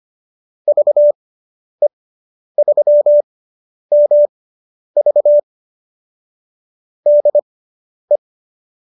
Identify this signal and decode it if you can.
Morse code, heard as a keyed tone, about 25 wpm, spaced as for 10 wpm, 590 Hz. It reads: VE3MV DE
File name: QSO220 10